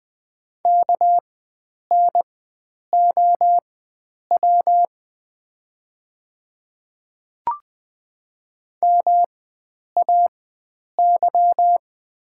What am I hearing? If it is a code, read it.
Morse code, heard as a keyed tone, about 20 wpm, spaced as for 9 wpm, 700 Hz. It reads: KNOW MAY
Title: 4x Character Spacing - Top 100 Words - 20wpm